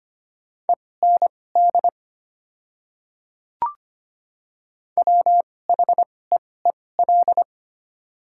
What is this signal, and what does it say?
Morse code, heard as a keyed tone, about 25 wpm, spaced as for 18 wpm, 700 Hz. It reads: END WHEEL
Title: 2x Character Spacing - Top 500 Words - 25wpm